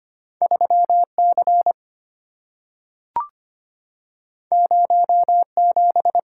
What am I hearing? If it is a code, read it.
Morse code, heard as a keyed tone, about 25 wpm, 690 Hz. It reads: 3C 07
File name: Two Letter-Number Combinations - 25wpm